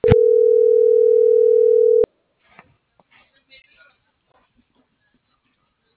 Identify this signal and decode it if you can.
Ringback tone (North America), 440 + 480 Hz.